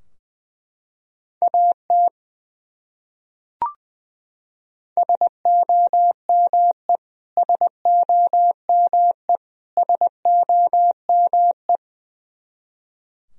Morse code, heard as a keyed tone, about 21 wpm, 700 Hz. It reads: AT SOME SOME SOME
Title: Top 100 Words Repeated 20wpm